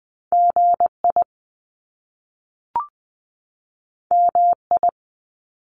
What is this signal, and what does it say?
Morse code, heard as a keyed tone, about 20 wpm, 700 Hz. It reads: G I M I
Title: Top Two Letter Combinations in Words 20wpm